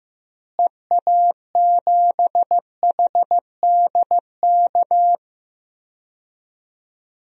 Morse code, heard as a keyed tone, about 15 wpm, 700 Hz. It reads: EA7HDK
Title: DXCC-100 Call Signs - 6 Characters - Repeated 3x - 15wpm